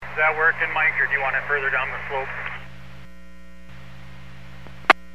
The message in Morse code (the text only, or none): none